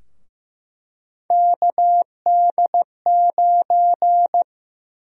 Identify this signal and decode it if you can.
Morse code, heard as a keyed tone, about 15 wpm, 700 Hz. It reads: KD9
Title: USA Call Sign Prefixes - Repeated 2x - 15wpm